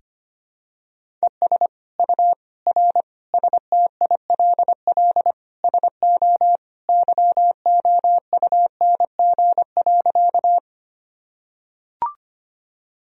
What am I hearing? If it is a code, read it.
Morse code, heard as a keyed tone, about 25 wpm, 700 Hz. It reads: ES U R STILL SO YOUNG.
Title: Sentences from Top 400 Words and Abbreviations 25wpm